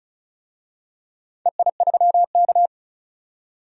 Morse code, heard as a keyed tone, about 34 wpm, 700 Hz. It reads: EI3K
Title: DXCC-100 Call Signs - 4 Characters - 35wpm